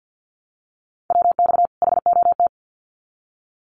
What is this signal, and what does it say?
Morse code, heard as a keyed tone, about 50 wpm, 700 Hz. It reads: WX HOT
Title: Common QSO Phrases 50wpm